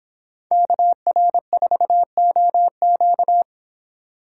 Morse code, heard as a keyed tone, about 26 wpm, 700 Hz. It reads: KR4OQ
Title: Speed Racing - USA 2x2 Call Signs 22wpm